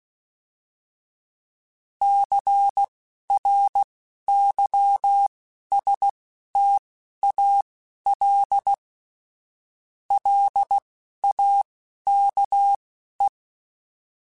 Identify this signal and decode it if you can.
Morse code, heard as a keyed tone, about 16 wpm, spaced as for 11 wpm, 800 Hz. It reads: CRYSTAL LAKE